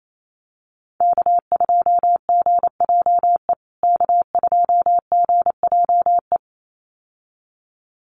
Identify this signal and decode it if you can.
Morse code, heard as a keyed tone, about 28 wpm, 700 Hz. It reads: K2GJE K2GJE
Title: USA Call Signs Repeated 2x 28wpm